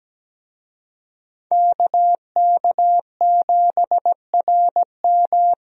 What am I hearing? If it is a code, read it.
Morse code, heard as a keyed tone, about 17 wpm, 700 Hz. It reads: KK7RM
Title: USA Call Signs - 5 Characters - 17wpm